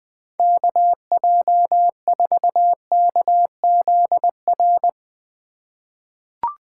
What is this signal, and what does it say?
Morse code, heard as a keyed tone, about 20 wpm, 700 Hz. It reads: KJ4KZR
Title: USA Call Signs - 6 Characters - 4x Character Spacing - 20wpm